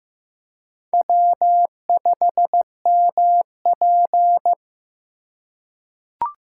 Morse code, heard as a keyed tone, about 16 wpm, 700 Hz. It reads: W5MP
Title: USA Call Signs - 4 Characters - 4x Character Spacing - 15wpm